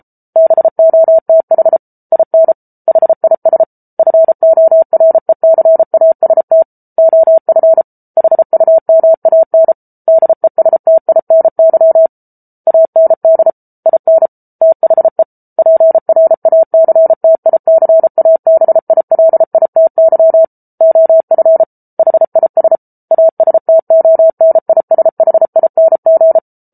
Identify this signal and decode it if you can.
Morse code, heard as a keyed tone, about 34 wpm, 650 Hz. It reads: BOTH IN HIS FORECAST OF HUMAN DESTINY AND IN THE PRACTICABILITY OF HIS ASTONISHING